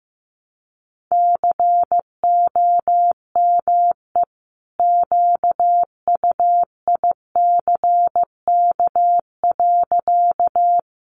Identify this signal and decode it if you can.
Morse code, heard as a keyed tone, about 15 wpm, 700 Hz. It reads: COME QUICK.
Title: Sentences from Top 500 Words 15wpm